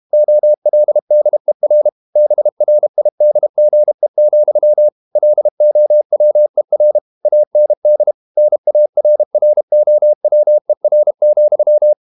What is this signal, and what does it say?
Morse code, heard as a keyed tone, about 32 wpm, 600 Hz. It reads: OLDER BRIDGE, LOWER AND NARROWER,
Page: Never 31wpm | CW med Gnister